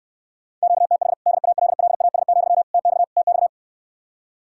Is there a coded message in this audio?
Yes: Morse code from 0.6 to 3.5 s, heard as a keyed tone, about 69 wpm, 700 Hz. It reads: QTH DALLAS, TX TX